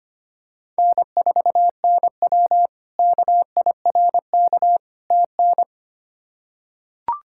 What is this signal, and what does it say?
Morse code, heard as a keyed tone, about 25 wpm, 700 Hz. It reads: N4NW KIRK TN
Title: K1USN SST QSO Practice - 25wpm